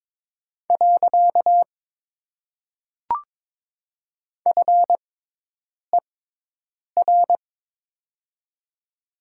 Morse code, heard as a keyed tone, about 22 wpm, 700 Hz. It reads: . F E R
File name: Top 90 QSO Elements - 6x Character Spacing - 22wpm